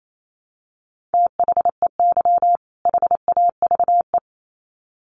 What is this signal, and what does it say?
Morse code, heard as a keyed tone, about 28 wpm, 700 Hz. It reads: THEY HAVE
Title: Sets of 2 Words 28wpm